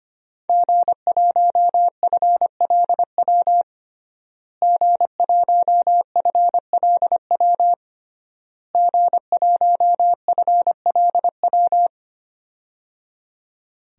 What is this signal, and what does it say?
Morse code, heard as a keyed tone, about 25 wpm, 700 Hz. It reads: G1FLW G1FLW G1FLW
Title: DXCC-100 Call Signs - 5 Characters - Repeated 3x - 25wpm